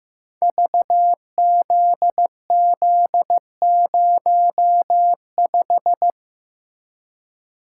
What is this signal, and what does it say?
Morse code, heard as a keyed tone, about 15 wpm, 700 Hz. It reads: VZZ05
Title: Code Groups - 5 Letter:Numbers - 15wpm